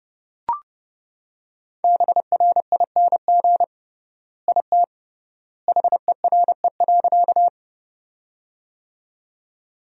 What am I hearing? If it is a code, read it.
Morse code, heard as a keyed tone, about 30 wpm, 700 Hz. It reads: BRING IT HERE.
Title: 3x Word-Spacing - Sentences from Top 500 Words - 30wpm